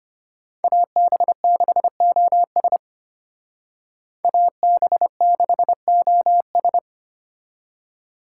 Morse code, heard as a keyed tone, about 33 wpm, 700 Hz. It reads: AB6OS AB6OS